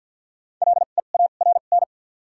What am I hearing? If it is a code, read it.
Morse code, heard as a keyed tone, about 53 wpm, 700 Hz. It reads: L E A R N
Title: 2x Character Spacing - Top 500 Words - 50wpm